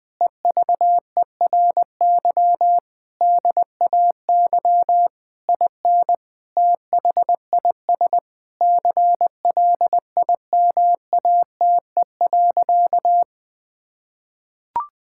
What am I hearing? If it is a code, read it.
Morse code, heard as a keyed tone, about 20 wpm, 700 Hz. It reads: EVERY DAY IN THIS CLIMATE.
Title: Sentences from Top 1600 Words 20wpm